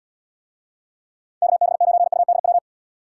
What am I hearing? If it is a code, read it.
Morse code, heard as a keyed tone, about 75 wpm, 700 Hz. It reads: KB9UDW